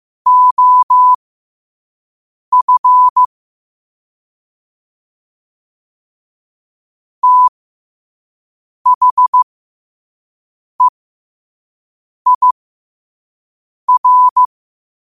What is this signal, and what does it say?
Morse code, heard as a keyed tone, about 15 wpm, spaced as for 5 wpm, 1000 Hz. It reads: OF THEIR
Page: Quotes for Tue, 12 Aug 2025 in Morse Code at 5 words per minute.